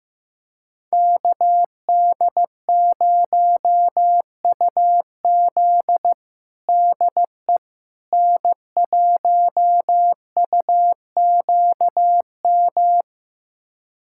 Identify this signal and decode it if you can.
Morse code, heard as a keyed tone, about 15 wpm, 700 Hz. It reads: KD0UZ DE N1UQM
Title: USA Callsign DE Callsign - 5 Characters - 15wpm